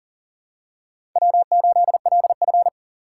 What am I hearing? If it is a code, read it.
Morse code, heard as a keyed tone, about 40 wpm, 700 Hz. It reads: W8LF
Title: USA Call Signs - 2x Character Spacing - 40wpm